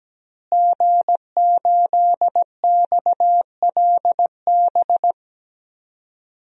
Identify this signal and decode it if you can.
Morse code, heard as a keyed tone, about 17 wpm, 700 Hz. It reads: G8XLB
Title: DXCC-100 Call Signs - 5 Characters - Repeated 3x - 17wpm